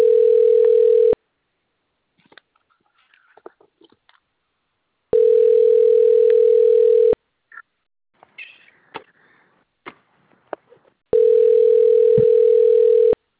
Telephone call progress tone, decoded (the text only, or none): ringback tone